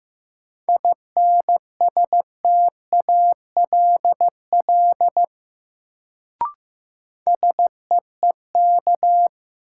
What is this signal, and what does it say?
Morse code, heard as a keyed tone, about 15 wpm, 700 Hz. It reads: INSTALL SEEK
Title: Sentences from Top 1600 Words - New Word Drill Alt - 15wpm